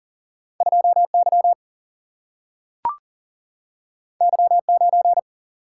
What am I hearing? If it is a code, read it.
Morse code, heard as a keyed tone, about 40 wpm, 700 Hz. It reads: 2 Y Y 9
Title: Single-Letter Single-Number 40wpm